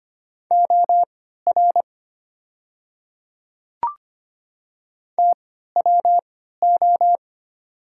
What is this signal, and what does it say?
Morse code, heard as a keyed tone, about 25 wpm, spaced as for 14 wpm, 700 Hz. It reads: OR TWO
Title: 3x Character Spacing - Top 100 Words - 25wpm